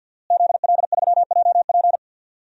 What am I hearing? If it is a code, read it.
Morse code, heard as a keyed tone, about 50 wpm, 700 Hz. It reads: ZL3JP
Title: DXCC-100 Call Signs - 5 Characters - Repeated 2x - 50wpm